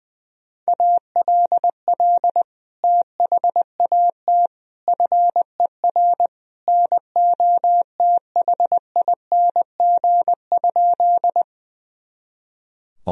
Morse code, heard as a keyed tone, about 20 wpm, 700 Hz. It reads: ALL THAT FER NOTHING?
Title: Sentences from Top 500 Words and Abbreviations 20wpm